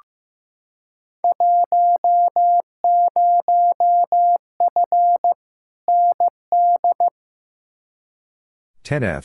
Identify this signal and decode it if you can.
Morse code, heard as a keyed tone, about 15 wpm, 700 Hz. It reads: 10F ND
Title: Field Day Exchanges Encore - 15wpm